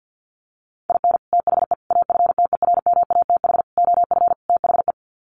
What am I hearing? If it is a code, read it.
Morse code, heard as a keyed tone, about 50 wpm, 700 Hz. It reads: IN THE AFTERMATH OF THE
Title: Sets of 5 Words 50wpm